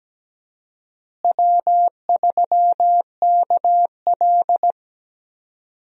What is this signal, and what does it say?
Morse code, heard as a keyed tone, about 17 wpm, 700 Hz. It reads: W3KL
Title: USA Call Signs - 4 Characters - Repeated 3x - 17wpm